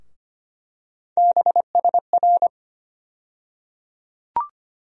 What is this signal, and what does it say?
Morse code, heard as a keyed tone, about 25 wpm, 700 Hz. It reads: BSR
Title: USA Call Sign Postfixes - 3 Letters 25wpm